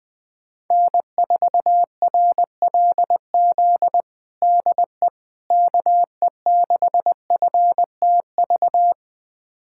Morse code, heard as a keyed tone, about 21 wpm, 700 Hz. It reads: N4RLZ DE KE6FTV
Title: USA Callsign DE Callsign 20wpm